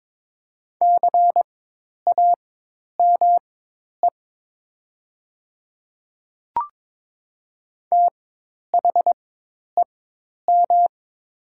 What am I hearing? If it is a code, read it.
Morse code, heard as a keyed tone, about 22 wpm, spaced as for 10 wpm, 700 Hz. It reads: CAME THEM
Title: 4x Character Spacing - Top 200 Words - 22wpm